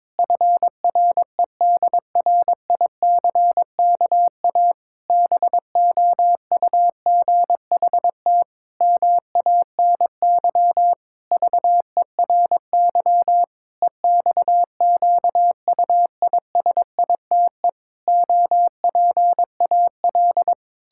Morse code, heard as a keyed tone, about 22 wpm, 700 Hz. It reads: FREDRICKA BOUGHT MANY VERY EXQUISITE OPAL
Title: Pangram Challenge v2 - 22wpm